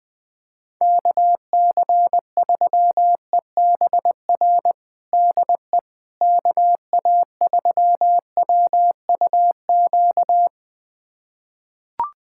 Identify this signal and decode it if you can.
Morse code, heard as a keyed tone, about 20 wpm, 700 Hz. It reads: KC3EBR DE KA3WUQ
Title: USA Callsign DE Callsign - 6 Characters - 20wpm